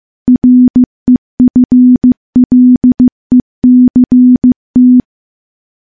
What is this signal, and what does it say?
Morse code, heard as a keyed tone, about 16 wpm, 250 Hz. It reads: REFLECT
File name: morse-training-15WPM